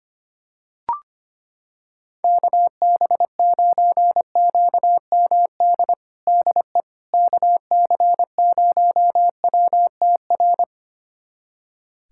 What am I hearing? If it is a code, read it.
Morse code, heard as a keyed tone, about 25 wpm, 700 Hz. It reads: KB9QMD DE KC0WTR
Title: USA Callsign DE Callsign 25wpm